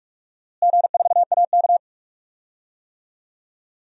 Morse code, heard as a keyed tone, about 46 wpm, 700 Hz. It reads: G4AK
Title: DXCC-100 Call Signs - 4 Characters - Repeated 2x - 45wpm